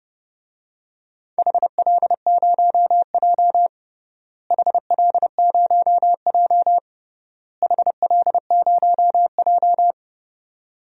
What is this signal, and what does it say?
Morse code, heard as a keyed tone, about 32 wpm, 700 Hz. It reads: HL0J HL0J HL0J
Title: DXCC-100 Call Signs - 4 Characters - Repeated 3x - 30wpm